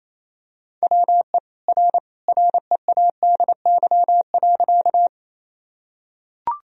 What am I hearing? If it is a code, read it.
Morse code, heard as a keyed tone, about 28 wpm, 700 Hz. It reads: WE R READY.
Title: Sentences from Top 400 Words and Abbreviations 28wpm